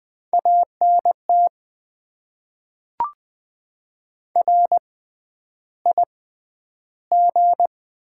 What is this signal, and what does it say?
Morse code, heard as a keyed tone, about 20 wpm, spaced as for 10 wpm, 700 Hz. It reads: ANT RIG
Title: Top 20 QSO Elements - 6x Character Spacing - 20wpm